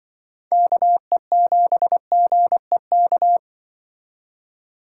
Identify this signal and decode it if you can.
Morse code, heard as a keyed tone, about 24 wpm, 700 Hz. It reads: KE7GEK
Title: Speed Racing - USA Call Signs 20wpm